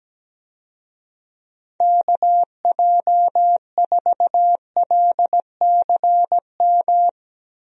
Morse code, heard as a keyed tone, about 17 wpm, 700 Hz. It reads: KJ4LCM
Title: USA Call Signs - 6 Characters - 17wpm